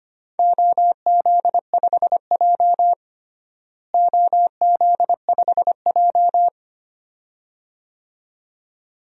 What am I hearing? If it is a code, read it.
Morse code, heard as a keyed tone, about 25 wpm, 700 Hz. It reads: OZ5J OZ5J
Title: DXCC-100 Call Signs - 4 Characters - Repeated 2x - 25wpm